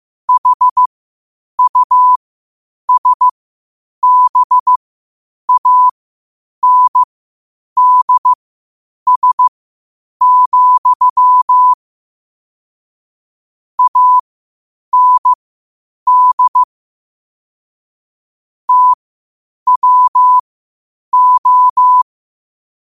Morse code, heard as a keyed tone, about 15 wpm, spaced as for 8 wpm, 1000 Hz. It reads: HUSBANDS, AND TWO
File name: qotdfiles / 8wpm